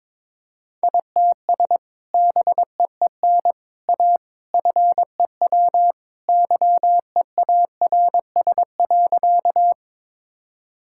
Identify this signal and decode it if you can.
Morse code, heard as a keyed tone, about 22 wpm, 700 Hz. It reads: ITS BEEN A FEW YEARS.
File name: practice / 22wpm / Sentences with 3-Word Phrases - 22wpm